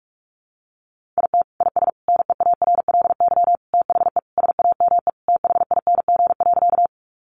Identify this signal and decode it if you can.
Morse code, heard as a keyed tone, about 44 wpm, 700 Hz. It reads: IT IS NEARLY THE SAME THING.
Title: Sentences from Top 1300 Words 45wpm